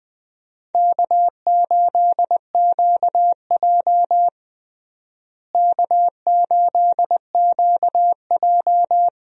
Morse code, heard as a keyed tone, about 21 wpm, 700 Hz. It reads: K8QJ K8QJ
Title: USA Call Signs - 4 Characters - Repeated 3x - 20wpm